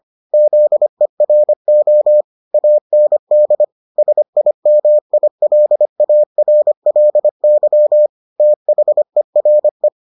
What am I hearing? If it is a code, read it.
Morse code, heard as a keyed tone, about 25 wpm, 600 Hz. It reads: ZERO AND SIMILARLY THERE